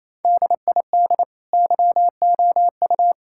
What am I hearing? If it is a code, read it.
Morse code, heard as a keyed tone, about 27 wpm, 700 Hz. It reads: DID YOU
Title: Sentences from Top 1000 Words - Review Part 6 - 28wpm